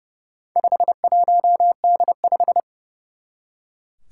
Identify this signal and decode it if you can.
Morse code, heard as a keyed tone, about 30 wpm, 700 Hz. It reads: 51D5
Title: Four Challenging Letter-Number Combinations 30wpm